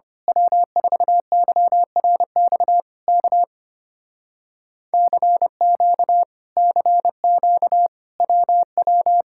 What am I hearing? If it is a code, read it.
Morse code, heard as a keyed tone, about 24 wpm, 700 Hz. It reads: W4YRX K CQ CQ WW
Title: Speed-Racing Common QSO Phrases - 25wpm